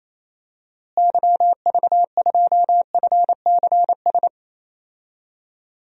Morse code, heard as a keyed tone, about 28 wpm, 700 Hz. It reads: YV2FCS
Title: DXCC-100 Call Signs - 6 Characters - Repeated 2x - 28wpm